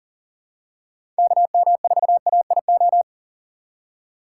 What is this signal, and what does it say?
Morse code, heard as a keyed tone, about 40 wpm, 700 Hz. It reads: KM4AIO
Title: USA Call Signs - 6 Characters - 4x Character Spacing - 40wpm